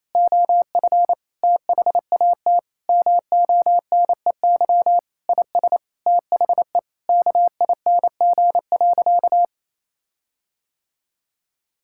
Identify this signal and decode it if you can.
Morse code, heard as a keyed tone, about 28 wpm, 700 Hz. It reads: OF THAT MONEY IS THE KING.
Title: Sentences from Top 1600 Words 28wpm